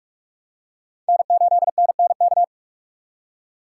Morse code, heard as a keyed tone, about 44 wpm, 700 Hz. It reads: N8NNK
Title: USA Call Signs - 5 Characters - 3x Character Spacing - 45wpm